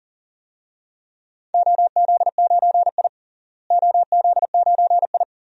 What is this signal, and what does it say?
Morse code, heard as a keyed tone, about 40 wpm, 700 Hz. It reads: OZ9I OZ9I
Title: DXCC-100 Call Signs - 4 Characters - Repeated 3x - 40wpm